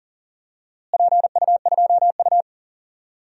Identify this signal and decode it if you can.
Morse code, heard as a keyed tone, about 38 wpm, 700 Hz. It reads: PU2U
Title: DXCC-100 Call Signs - 4 Characters - Repeated 3x - 40wpm